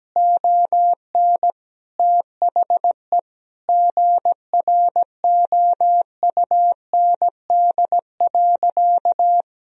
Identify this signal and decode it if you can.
Morse code, heard as a keyed tone, about 17 wpm, 700 Hz. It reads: ON THE GROUND.
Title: Sentences from Top 1000 Words - Review Part 2 - 17wpm